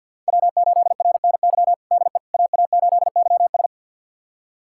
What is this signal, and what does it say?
Morse code, heard as a keyed tone, about 51 wpm, 700 Hz. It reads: W8RNY DE AA8YS
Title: USA Callsign DE Callsign 50wpm